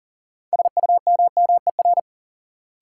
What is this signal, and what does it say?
Morse code, heard as a keyed tone, about 37 wpm, 700 Hz. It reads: SUMMER
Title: Sentences from Top 1100 Words - New Word Drill Alt - 40wpm